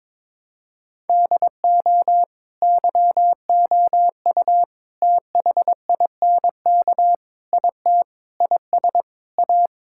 Morse code, heard as a keyed tone, about 22 wpm, 700 Hz. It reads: DO YOU THINK IT IS A
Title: Sentences from Top 1000 Words - Review Part 6 - 22wpm